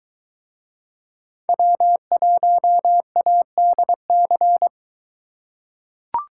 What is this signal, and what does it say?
Morse code, heard as a keyed tone, about 23 wpm, 700 Hz. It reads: W1ADC